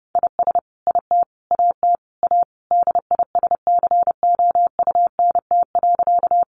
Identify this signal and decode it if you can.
Morse code, heard as a keyed tone, about 30 wpm, 700 Hz. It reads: IS IT AT A DISCOUNT.
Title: Sentences from Top 1200 Words 30wpm